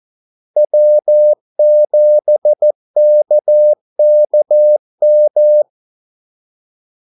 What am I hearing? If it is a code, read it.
Morse code, heard as a keyed tone, about 14 wpm, 600 Hz. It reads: W7KKM
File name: Callsigns-14wpm